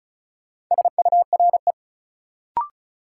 Morse code, heard as a keyed tone, about 35 wpm, 700 Hz. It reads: SURE
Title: Sentences from Top 300 Words - New Word Drill - 35wpm